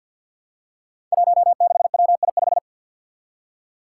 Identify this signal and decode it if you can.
Morse code, heard as a keyed tone, about 50 wpm, 700 Hz. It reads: 16WI5
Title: Five Challenging Letter-Number Combinations 50wpm